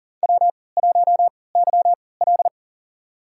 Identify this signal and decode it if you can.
Morse code, heard as a keyed tone, about 37 wpm, spaced as for 22 wpm, 700 Hz. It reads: W1YL
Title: USA Call Signs - 3x Character Spacing - 40wpm